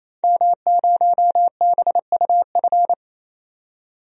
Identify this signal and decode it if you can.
Morse code, heard as a keyed tone, about 28 wpm, 700 Hz. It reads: M0BUF
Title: DXCC-100 Call Signs - 5 Characters - 28wpm